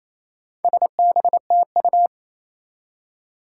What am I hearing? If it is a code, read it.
Morse code, heard as a keyed tone, about 28 wpm, 700 Hz. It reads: SBTU
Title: Four Challenging Letter-Number Combinations 28wpm